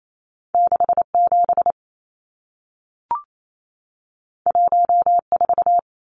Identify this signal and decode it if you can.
Morse code, heard as a keyed tone, about 28 wpm, 700 Hz. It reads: 67 14
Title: Numbers - 2-Digit 28wpm